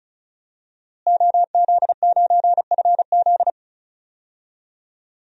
Morse code, heard as a keyed tone, about 36 wpm, 700 Hz. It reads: OZ9FZ
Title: DXCC-100 Call Signs - 5 Characters - Repeated 3x - 35wpm